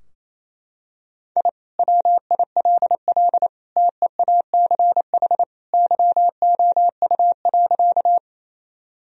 Morse code, heard as a keyed tone, about 27 wpm, 700 Hz. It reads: I WILL TEACH YOU.
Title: Sentences from Top 1000 Words - Review Part 2 - 28wpm